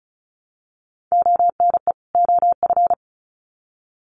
Morse code, heard as a keyed tone, about 34 wpm, 700 Hz. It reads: ONE OF
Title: Sets of 2 Words 35wpm